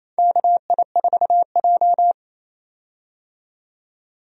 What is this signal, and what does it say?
Morse code, heard as a keyed tone, about 28 wpm, 700 Hz. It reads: KI4J
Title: USA Call Signs - 4 Characters - 28wpm